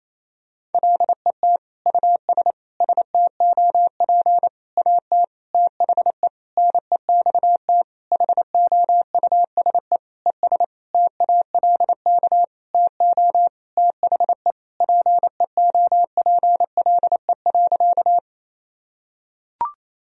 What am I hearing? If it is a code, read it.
Morse code, heard as a keyed tone, about 28 wpm, 700 Hz. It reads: LET US STOP AT THE NEXT HOUSE ES TALK TO THE PEOPLE.
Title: Sentences from Top 400 Words and Abbreviations 28wpm